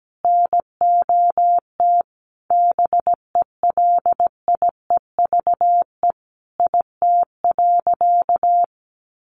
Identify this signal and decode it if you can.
Morse code, heard as a keyed tone, about 17 wpm, 700 Hz. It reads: NOT BELIEVE IT.
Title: Sentences from Top 1000 Words 17wpm